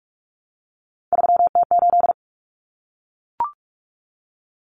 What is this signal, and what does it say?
Morse code, heard as a keyed tone, about 44 wpm, 700 Hz. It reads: 3T8
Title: Common Cut Numbers - 3-Digit 45wpm